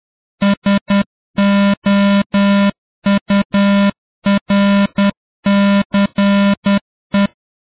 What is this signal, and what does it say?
Morse code, heard as a keyed tone, about 10 wpm, 200 Hz. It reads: SOURCE